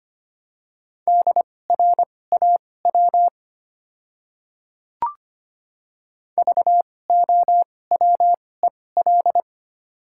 Morse code, heard as a keyed tone, about 25 wpm, spaced as for 18 wpm, 700 Hz. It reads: DRAW VOWEL
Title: 2x Character Spacing - Top 500 Words - 25wpm